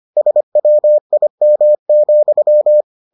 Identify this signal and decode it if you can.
Morse code, heard as a keyed tone, about 25 wpm, 600 Hz. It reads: SWIM,